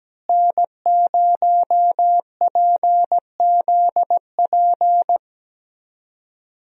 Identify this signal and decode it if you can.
Morse code, heard as a keyed tone, about 17 wpm, 700 Hz. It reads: N0PZP
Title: USA Call Signs - 5 Characters - Repeated 3x - 17wpm